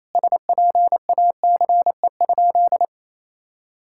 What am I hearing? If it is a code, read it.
Morse code, heard as a keyed tone, about 28 wpm, 700 Hz. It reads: SPACE?
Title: Sentences from Top 1000 Words - Review Part 5 - 28wpm